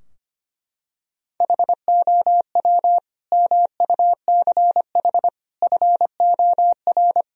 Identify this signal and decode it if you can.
Morse code, heard as a keyed tone, about 25 wpm, 700 Hz. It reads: HOW MUCH FOR
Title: Sentences from Top 1000 Words - Review Part 3 - 25wpm